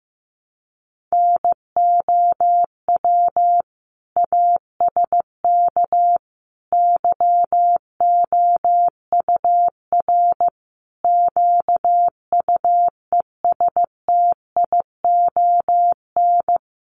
Morse code, heard as a keyed tone, about 14 wpm, 700 Hz. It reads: NOW ASK YOUR QUESTION
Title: Sentences from Top 1100 Words 15wpm